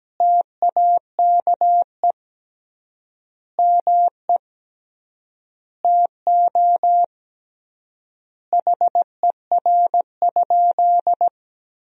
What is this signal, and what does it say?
Morse code, heard as a keyed tone, about 17 wpm, 700 Hz. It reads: TAKE ME TO HER?
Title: 3x Word-Spacing - Sentences from Top 200 Words - 17wpm